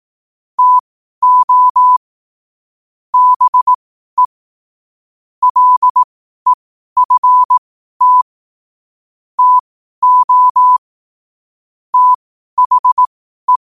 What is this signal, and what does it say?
Morse code, heard as a keyed tone, about 18 wpm, spaced as for 12 wpm, 1000 Hz. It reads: TO BE LEFT TO THE
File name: qotdfiles / 12wpm